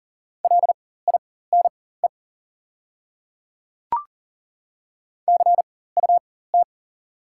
Morse code, heard as a keyed tone, about 40 wpm, spaced as for 19 wpm, 700 Hz. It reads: LINE CUT
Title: 4x Character Spacing - Top 300 Words - 40wpm